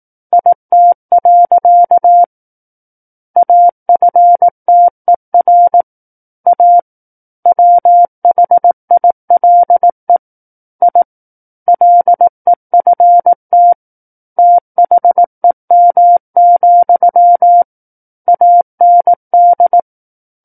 Morse code, heard as a keyed tone, about 18 wpm, 700 Hz. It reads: IT. AFTER A WHILE I LEFT THEM, AND
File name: 18 WPM